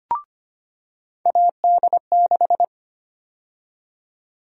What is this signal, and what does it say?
Morse code, heard as a keyed tone, about 25 wpm, 700 Hz. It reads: AD6
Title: USA Call Sign Prefixes - 25wpm